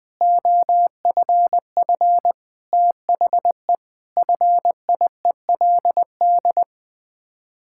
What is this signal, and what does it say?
Morse code, heard as a keyed tone, about 21 wpm, 700 Hz. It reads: OFF THE FIELD
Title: Sets of 6 Words - 20wpm